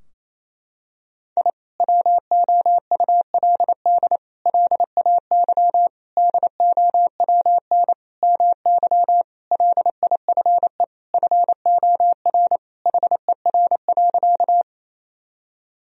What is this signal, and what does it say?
Morse code, heard as a keyed tone, about 28 wpm, 700 Hz. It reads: I WOULD LAY DOWN MY LIFE FOR HER.
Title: Sentences from Top 1000 Words - Review Part 2 - 28wpm